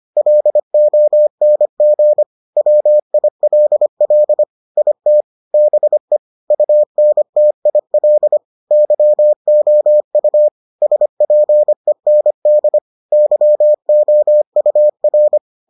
Morse code, heard as a keyed tone, about 25 wpm, 600 Hz. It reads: LONG WILL IT BE UNTIL YOU SPEND YOUR